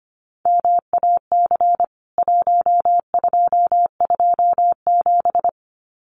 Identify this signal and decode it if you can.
Morse code, heard as a keyed tone, about 25 wpm, 700 Hz. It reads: MAC 1227
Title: CWT Exchange 25wpm